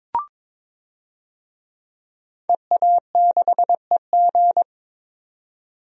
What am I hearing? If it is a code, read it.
Morse code, heard as a keyed tone, about 22 wpm, 700 Hz. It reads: EA6EG
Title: DXCC-100 Call Signs - 5 Characters - 22wpm